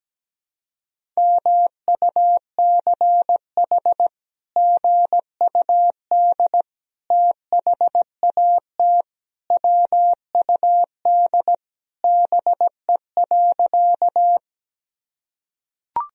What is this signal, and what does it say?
Morse code, heard as a keyed tone, about 17 wpm, 700 Hz. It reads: MUCH GUD THAT WUD BE.
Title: Sentences from Top 200 Words and Abbreviations 17wpm